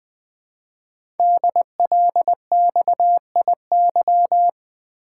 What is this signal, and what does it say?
Morse code, heard as a keyed tone, about 20 wpm, 700 Hz. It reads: DLXIY
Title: Code Groups - 5 Letters - 20wpm